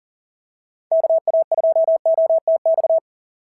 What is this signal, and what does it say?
Morse code, heard as a keyed tone, about 40 wpm, 690 Hz. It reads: KA2OTX
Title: SOTA QSO Practice 40wpm